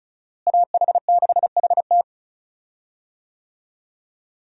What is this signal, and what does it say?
Morse code, heard as a keyed tone, about 36 wpm, 700 Hz. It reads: AH6HT
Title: USA Call Signs - 5 Characters - Repeated 3x - 35wpm